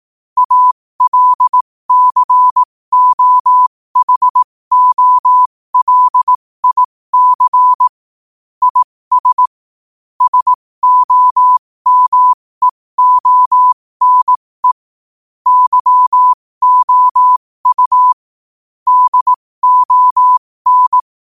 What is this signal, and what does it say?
Morse code, heard as a keyed tone, about 18 wpm, spaced as for 15 wpm, 1000 Hz. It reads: ALCOHOLIC IS SOMEONE YOU DON
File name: qotdfiles / 15wpm